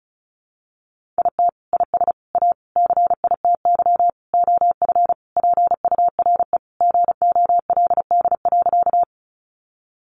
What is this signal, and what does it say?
Morse code, heard as a keyed tone, about 34 wpm, 700 Hz. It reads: IT IS A CITY OF PURE GOLD.
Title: Sentences from Top 1500 Words 35wpm